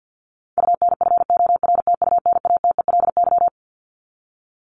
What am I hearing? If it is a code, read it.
Morse code, heard as a keyed tone, about 50 wpm, 700 Hz. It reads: UNFORTUNATELY
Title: Top 1751-2000 Words 50wpm